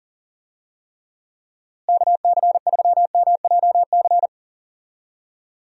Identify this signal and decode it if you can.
Morse code, heard as a keyed tone, about 40 wpm, 700 Hz. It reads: KC3MJC
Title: USA Call Signs - 6 Characters - Repeated 3x - 40wpm